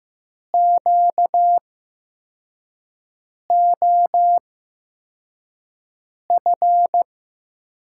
Morse code, heard as a keyed tone, about 15 wpm, spaced as for 4 wpm, 700 Hz. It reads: QOF
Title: USA Call Signs - 6 Characters - 8x Character Spacing - 15wpm